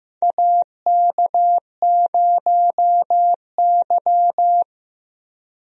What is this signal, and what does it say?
Morse code, heard as a keyed tone, about 15 wpm, 700 Hz. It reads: AK0Y
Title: USA Call Signs - 4 Characters - 15wpm